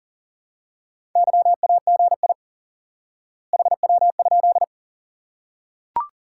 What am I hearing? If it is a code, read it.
Morse code, heard as a keyed tone, about 40 wpm, 700 Hz. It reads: YAGI HW?
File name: IQE Course 3 - Lesson 89 - YAGI - Two Elements - 40wpm